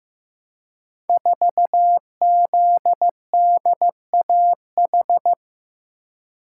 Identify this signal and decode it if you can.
Morse code, heard as a keyed tone, about 15 wpm, 700 Hz. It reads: 4ZDAH
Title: Code Groups - 5 Letter:Numbers - 15wpm